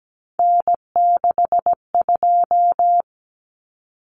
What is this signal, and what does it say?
Morse code, heard as a keyed tone, about 17 wpm, 700 Hz. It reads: N62
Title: Common Cut Numbers - 3-Digit 17wpm